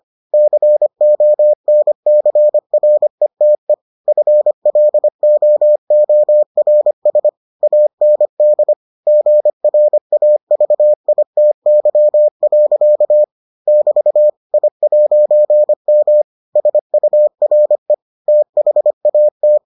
Morse code, heard as a keyed tone, about 25 wpm, 600 Hz. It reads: CONCRETE FLOORS AND GRAVITY. <BT> I'M SURE THAT